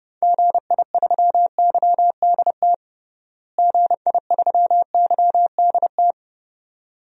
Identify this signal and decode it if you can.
Morse code, heard as a keyed tone, about 32 wpm, 700 Hz. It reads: GI3YDT GI3YDT
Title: DXCC-100 Call Signs - 6 Characters - Repeated 2x - 30wpm